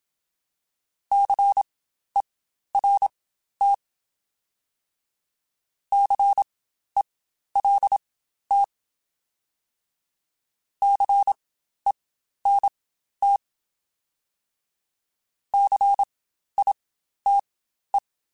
Morse code, heard as a keyed tone, about 27 wpm, spaced as for 12 wpm, 800 Hz. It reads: CERT CELT CENT CITE